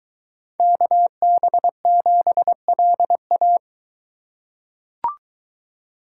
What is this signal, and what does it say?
Morse code, heard as a keyed tone, about 22 wpm, 700 Hz. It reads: KB7LA